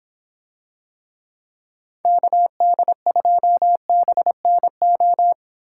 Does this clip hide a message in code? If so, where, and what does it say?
Yes: Morse code from 2.0 to 5.3 s, heard as a keyed tone, about 25 wpm, 700 Hz. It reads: KD2BNO